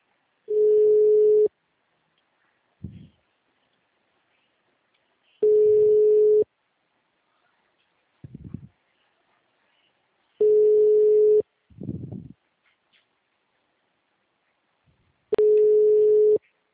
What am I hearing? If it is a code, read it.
Ringback tone (Europe, 425 Hz).
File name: out- -2003